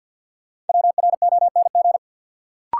Morse code, heard as a keyed tone, about 53 wpm, 700 Hz. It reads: WRONG
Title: Sentences from Top 900 Words - New Word Drill - 50wpm